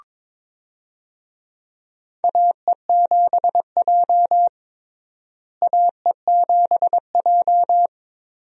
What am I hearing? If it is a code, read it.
Morse code, heard as a keyed tone, about 22 wpm, 700 Hz. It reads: AE7J AE7J
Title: USA Call Signs - 4 Characters - Repeated 2x - 22wpm